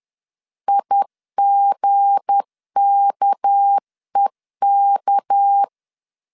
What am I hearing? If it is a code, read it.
Morse code, heard as a keyed tone, about 10 wpm, 800 Hz. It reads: IGKEK
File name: LEZIONI / 7 GF9Z / 7 G 10wpm 2501